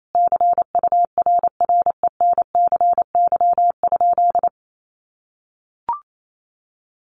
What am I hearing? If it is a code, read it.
Morse code, heard as a keyed tone, about 28 wpm, 700 Hz. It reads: CURRENCY?
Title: Sentences from Top 1500 Words 28wpm